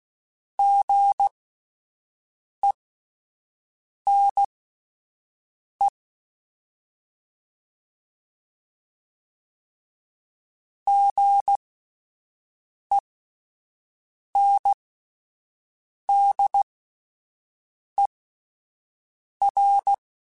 Morse code, heard as a keyed tone, about 16 wpm, spaced as for 5 wpm, 800 Hz. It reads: GENE GENDER
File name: Spaced Alphas / Spaced Alphas mp3s / Gs sp @10